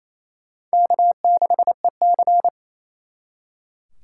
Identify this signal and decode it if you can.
Morse code, heard as a keyed tone, about 34 wpm, 700 Hz. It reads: K6EC
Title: Speed Racing - USA Call Signs 28wpm